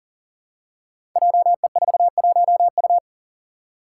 Morse code, heard as a keyed tone, about 38 wpm, 690 Hz. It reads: JE41U
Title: Five Challenging Letter-Number Combinations 40wpm